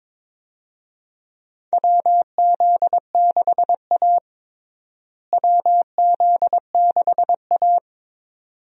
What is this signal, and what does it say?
Morse code, heard as a keyed tone, about 22 wpm, 700 Hz. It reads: WZ6A WZ6A
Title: USA Call Signs - 4 Characters - Repeated 2x - 22wpm